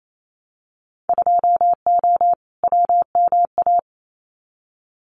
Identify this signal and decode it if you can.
Morse code, heard as a keyed tone, about 28 wpm, 690 Hz. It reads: 2O WMA
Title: Winter Field Day Exchanges Encore 28wpm